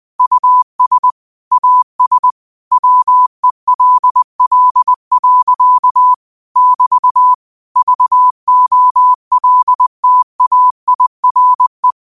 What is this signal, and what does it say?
Morse code, heard as a keyed tone, about 20 wpm, 1000 Hz. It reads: US AS WELL. <BT> VOLTAIRE
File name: qotdfiles / 20wpm